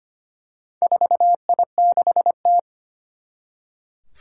Morse code, heard as a keyed tone, about 25 wpm, 700 Hz. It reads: 4I6T
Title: Four Challenging Letter-Number Combinations 25wpm